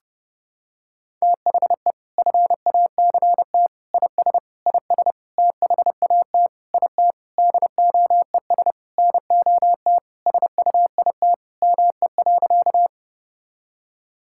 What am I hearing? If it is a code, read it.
Morse code, heard as a keyed tone, about 31 wpm, 700 Hz. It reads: THE FACT IS IS THAT IT DOES NOT SUIT ME.
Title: Sentences from Top 1600 Words 30wpm